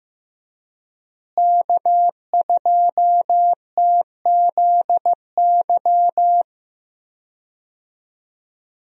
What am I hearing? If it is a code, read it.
Morse code, heard as a keyed tone, about 15 wpm, 700 Hz. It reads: K2TZY
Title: USA Call Signs - 5 Characters - Repeated 3x - 15wpm